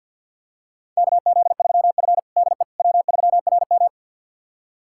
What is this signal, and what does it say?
Morse code, heard as a keyed tone, about 52 wpm, 700 Hz. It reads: KZ3F DE W3RM
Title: USA Callsign DE Callsign - 4 Characters - 50wpm